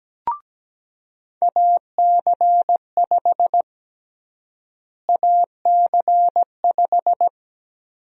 Morse code, heard as a keyed tone, about 17 wpm, 700 Hz. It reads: AC5 AC5
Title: USA Call Sign Prefixes - Repeated 2x - 17wpm